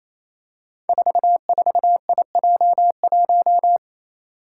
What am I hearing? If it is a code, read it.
Morse code, heard as a keyed tone, about 27 wpm, 700 Hz. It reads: 44IJ1
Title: Five Challenging Letter-Number Combinations 28wpm